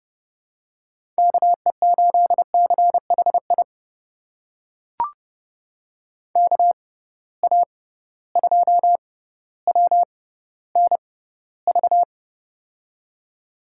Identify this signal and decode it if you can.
Morse code, heard as a keyed tone, about 30 wpm, spaced as for 15 wpm, 700 Hz. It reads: KE8CHI KA2WNV
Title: USA Call Signs - 6 Characters - 6x Character Spacing - 30wpm